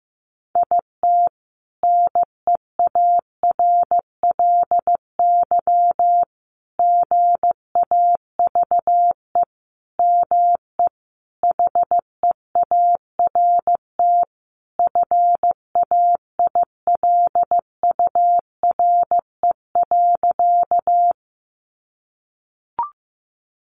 Morse code, heard as a keyed tone, about 15 wpm, 700 Hz. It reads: IT NEARLY GAVE ME HEART FAILURE.
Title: Sentences from Top 1300 Words 15wpm